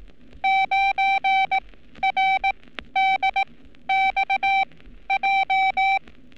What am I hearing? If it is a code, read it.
Morse code, heard as a keyed tone, about 18 wpm, spaced as for 12 wpm, 780 Hz. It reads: 9RDXJ
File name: training / NAVPERS 11224-11235 / NAVPERS 11227 RA RB